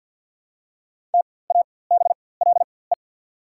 Morse code, heard as a keyed tone, about 48 wpm, spaced as for 23 wpm, 700 Hz. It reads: TABLE
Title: 4x Character Spacing - Top 500 Words - 50wpm